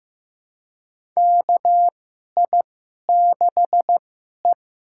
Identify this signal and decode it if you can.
Morse code, heard as a keyed tone, about 15 wpm, spaced as for 11 wpm, 700 Hz. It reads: KI6E
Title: USA Call Signs - 4 Characters - 2x Character Spacing - 15wpm